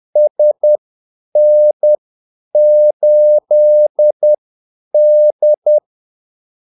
Morse code, heard as a keyed tone, about 10 wpm, spaced as for 8 wpm, 600 Hz. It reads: SN8D
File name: Callsigns-10-8wpm